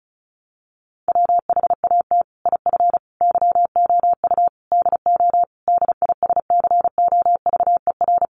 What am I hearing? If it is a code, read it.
Morse code, heard as a keyed tone, about 34 wpm, 700 Hz. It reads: WHAT IF YOU DO DISCOVER
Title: Sentences from Top 1100 Words 35wpm